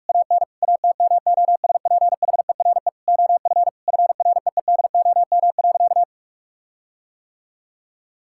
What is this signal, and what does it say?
Morse code, heard as a keyed tone, about 46 wpm, 700 Hz. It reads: AN ATMOSPHERE OF FREEDOM.
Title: Sentences from Top 1600 Words 45wpm